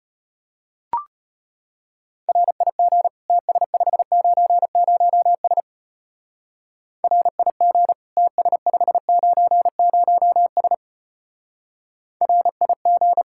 Morse code, heard as a keyed tone, about 31 wpm, 700 Hz. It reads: RIG TS590S RIG TS590S RIG
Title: Speed-Racing Common QSO Phrases - 25wpm